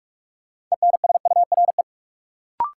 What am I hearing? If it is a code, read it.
Morse code, heard as a keyed tone, about 42 wpm, 700 Hz. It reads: ENSURE
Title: Sentences from Top 1100 Words - New Word Drill Alt - 45wpm